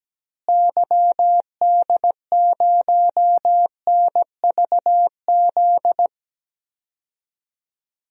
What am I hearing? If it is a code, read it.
Morse code, heard as a keyed tone, about 17 wpm, 700 Hz. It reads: YD0NVZ
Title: DXCC-100 Call Signs - 6 Characters - Repeated 3x - 17wpm